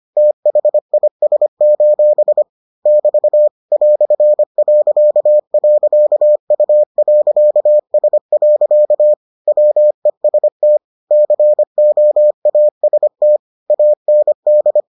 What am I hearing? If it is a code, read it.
Morse code, heard as a keyed tone, about 25 wpm, 600 Hz. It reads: THIS: <BT> "..U.S. WEST COAST AND